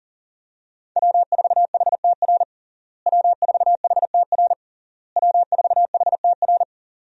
Morse code, heard as a keyed tone, about 40 wpm, 700 Hz. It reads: W4HTR W4HTR W4HTR
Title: USA Call Signs - 5 Characters - Repeated 3x - 40wpm